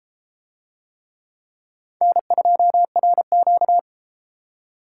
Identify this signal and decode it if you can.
Morse code, heard as a keyed tone, about 32 wpm, 700 Hz. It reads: N2RQ